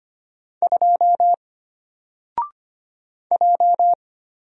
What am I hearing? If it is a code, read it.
Morse code, heard as a keyed tone, about 25 wpm, 700 Hz. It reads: 2 J
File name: Single-Letter-Number-25wpm